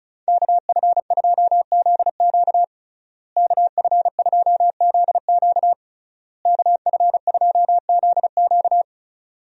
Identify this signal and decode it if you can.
Morse code, heard as a keyed tone, about 36 wpm, 700 Hz. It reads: KF2ZQ KF2ZQ KF2ZQ
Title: USA Call Signs - 5 Characters - Repeated 3x - 35wpm